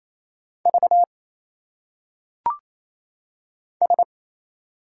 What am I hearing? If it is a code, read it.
Morse code, heard as a keyed tone, about 28 wpm, 700 Hz. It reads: V S
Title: Single Letters - Phonetic Pronunciation 28wpm